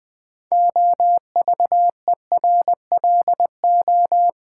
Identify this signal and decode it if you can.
Morse code, heard as a keyed tone, about 21 wpm, 700 Hz. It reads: OVERLO
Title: Build A Word - 3 Letters At A Time - 8-Letter Words - 20wpm